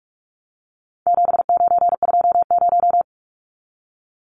Morse code, heard as a keyed tone, about 44 wpm, 700 Hz. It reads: 7920
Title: Numbers - 4-Digit 45wpm